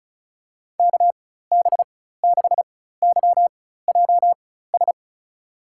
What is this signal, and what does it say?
Morse code, heard as a keyed tone, about 36 wpm, spaced as for 16 wpm, 700 Hz. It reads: KB6YJS
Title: USA Call Signs - 6 Characters - 4x Character Spacing - 35wpm